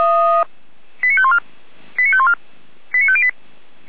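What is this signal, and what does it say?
Five-tone selective call (ZVEI), heard as quick runs of five tones, tones of about 70 ms. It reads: 89514 89514 89508
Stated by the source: ZVEI five-tone selective call received by SDR.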